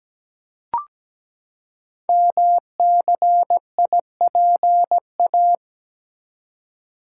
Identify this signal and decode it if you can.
Morse code, heard as a keyed tone, about 17 wpm, 700 Hz. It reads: MCIPA
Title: Code Groups - 5 Letters - 17wpm